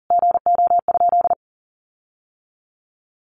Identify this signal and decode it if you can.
Morse code, heard as a keyed tone, about 40 wpm, 700 Hz. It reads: GO?
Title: Sentences from Top 1300 Words 40wpm